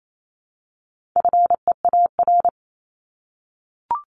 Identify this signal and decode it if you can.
Morse code, heard as a keyed tone, about 28 wpm, 700 Hz. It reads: FEAR
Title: Top 1751-2000 Words 28wpm